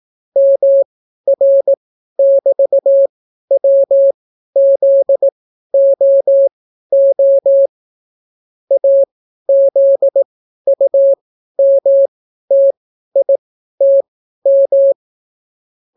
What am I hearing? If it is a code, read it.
Morse code, heard as a keyed tone, about 18 wpm, spaced as for 12 wpm, 550 Hz. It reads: MR<BT>WZOO AZUMTITM